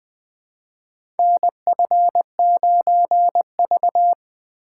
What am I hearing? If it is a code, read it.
Morse code, heard as a keyed tone, about 20 wpm, 700 Hz. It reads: NF9V
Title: USA Call Signs - 4 Characters - 3x Character Spacing - 20wpm